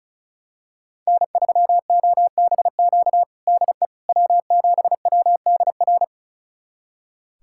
Morse code, heard as a keyed tone, about 36 wpm, 700 Hz. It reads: N3OBQ DE W7WDR
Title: USA Callsign DE Callsign - 5 Characters - 35wpm